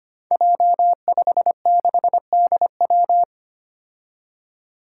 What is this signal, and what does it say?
Morse code, heard as a keyed tone, about 25 wpm, 700 Hz. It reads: J56DW
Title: Five Challenging Letter-Number Combinations 25wpm